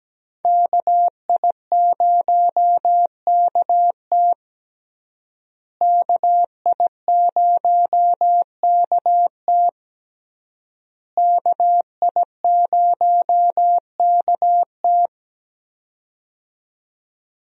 Morse code, heard as a keyed tone, about 17 wpm, 700 Hz. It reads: KI0KT KI0KT KI0KT
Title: USA Call Signs - 5 Characters - Repeated 3x - 17wpm